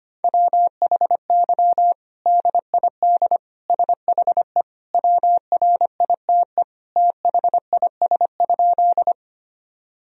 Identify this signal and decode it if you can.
Morse code, heard as a keyed tone, about 25 wpm, 700 Hz. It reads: WHY DID SHE WRITE THIS?
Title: Sentences from Top 1000 Words - Review Part 1 - 25wpm